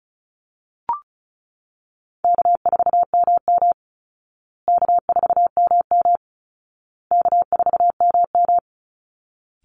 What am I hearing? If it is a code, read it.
Morse code, heard as a keyed tone, about 34 wpm, 700 Hz. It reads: K4MM K4MM K4MM
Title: Easy Call Signs Repeated 3x - 35wpm